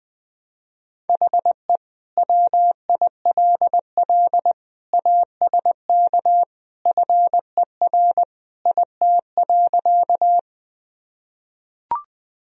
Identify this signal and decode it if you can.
Morse code, heard as a keyed tone, about 20 wpm, 700 Hz. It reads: HE WILL ASK FER IT.
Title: Sentences from Top 200 Words and Abbreviations 20wpm